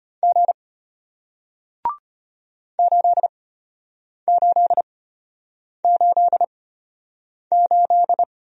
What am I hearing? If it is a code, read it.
Morse code, heard as a keyed tone, about 34 wpm, spaced as for 8 wpm, 700 Hz. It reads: G 8888